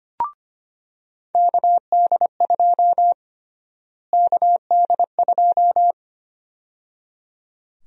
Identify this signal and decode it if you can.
Morse code, heard as a keyed tone, about 25 wpm, 700 Hz. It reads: KD2 KD2
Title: USA Call Sign Prefixes - Repeated 2x - 25wpm